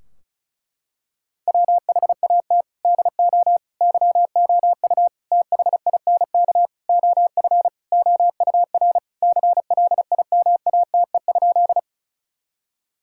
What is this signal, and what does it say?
Morse code, heard as a keyed tone, about 34 wpm, 700 Hz. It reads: WHAT DO YOU THINK OF OUR CLIMATE?
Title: Sentences from Top 1600 Words 35wpm